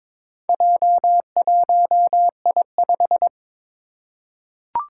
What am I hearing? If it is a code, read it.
Morse code, heard as a keyed tone, about 22 wpm, 700 Hz. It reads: J1I5
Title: Four Challenging Letter-Number Combinations 22wpm